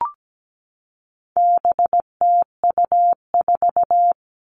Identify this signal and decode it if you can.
Morse code, heard as a keyed tone, about 17 wpm, 700 Hz. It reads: BTU4
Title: All Cut Numbers - 4-Digit 17wpm